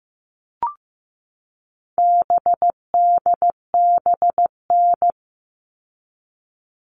Morse code, heard as a keyed tone, about 15 wpm, 700 Hz. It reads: BDBN
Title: All Cut Numbers - 4-Digit 15wpm